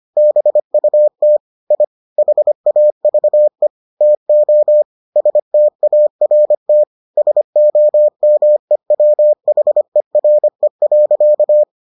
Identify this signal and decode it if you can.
Morse code, heard as a keyed tone, about 25 wpm, 600 Hz. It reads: BUT I HAVE TO START SOMEWHERE.